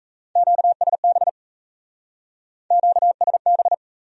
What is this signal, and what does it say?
Morse code, heard as a keyed tone, about 40 wpm, 700 Hz. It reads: QSB QSB